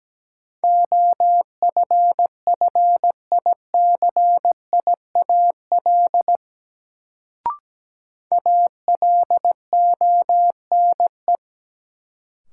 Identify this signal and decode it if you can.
Morse code, heard as a keyed tone, about 17 wpm, 700 Hz. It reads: OFFICIAL ALONE
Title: Sentences from Top 1000 Words - New Word Drill - 17wpm